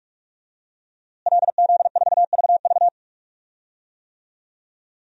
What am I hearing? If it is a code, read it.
Morse code, heard as a keyed tone, about 44 wpm, 700 Hz. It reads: LZ4VV
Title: DXCC-100 Call Signs - 5 Characters - Repeated 2x - 45wpm